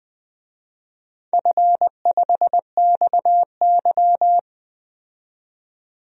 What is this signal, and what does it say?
Morse code, heard as a keyed tone, about 20 wpm, 700 Hz. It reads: F5XY
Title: DXCC-100 Call Signs - 4 Characters - 20wpm